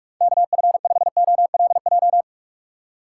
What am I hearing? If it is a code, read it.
Morse code, heard as a keyed tone, about 47 wpm, 700 Hz. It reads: KF5OLJ
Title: USA Call Signs - 6 Characters - 2x Character Spacing - 45wpm